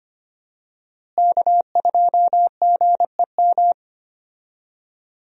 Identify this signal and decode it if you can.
Morse code, heard as a keyed tone, about 25 wpm, 700 Hz. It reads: K2GEM
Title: USA Call Signs - 5 Characters - Repeated 2x - 25wpm